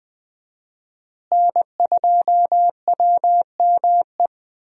Morse code, heard as a keyed tone, about 20 wpm, 700 Hz. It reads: N2WME
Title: USA Call Signs - 5 Characters - Repeated 2x - 20wpm